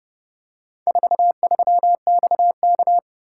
Morse code, heard as a keyed tone, about 27 wpm, 700 Hz. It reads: 43XK